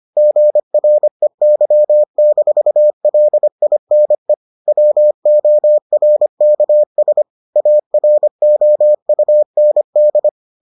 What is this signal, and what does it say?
Morse code, heard as a keyed tone, about 25 wpm, 600 Hz. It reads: GREY-LINE WORKS AROUND